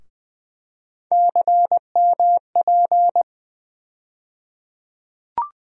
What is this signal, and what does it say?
Morse code, heard as a keyed tone, about 21 wpm, 700 Hz. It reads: CMP
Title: USA Call Sign Postfixes - 3 Letters 20wpm